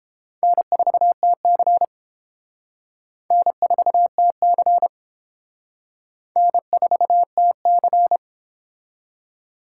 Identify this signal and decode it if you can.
Morse code, heard as a keyed tone, about 29 wpm, 700 Hz. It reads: N4TC N4TC N4TC